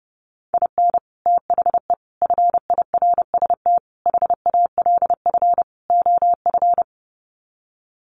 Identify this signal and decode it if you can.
Morse code, heard as a keyed tone, about 30 wpm, 700 Hz. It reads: IN THE FIRST HALF OF
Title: Sets of 5 Words 30wpm